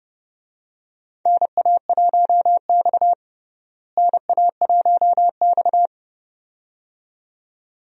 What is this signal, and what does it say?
Morse code, heard as a keyed tone, about 30 wpm, 700 Hz. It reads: NA1X NA1X
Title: USA Call Signs - 4 Characters - Repeated 2x - 30wpm